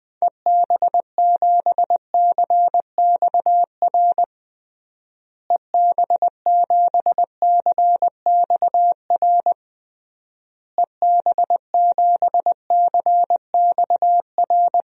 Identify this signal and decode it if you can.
Morse code, heard as a keyed tone, about 20 wpm, 700 Hz. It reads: EB7CXR EB7CXR EB7CXR
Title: DXCC-100 Call Signs - 6 Characters - Repeated 3x - 20wpm